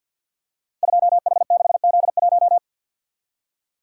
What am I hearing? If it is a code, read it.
Morse code, heard as a keyed tone, about 52 wpm, 700 Hz. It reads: 2H6Z1
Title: Code Groups - 5 Letter:Numbers - 50wpm